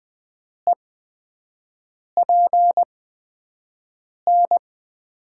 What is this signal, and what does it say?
Morse code, heard as a keyed tone, about 20 wpm, spaced as for 5 wpm, 700 Hz. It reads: EPN
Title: USA Call Signs - 8x Character Spacing - 20wpm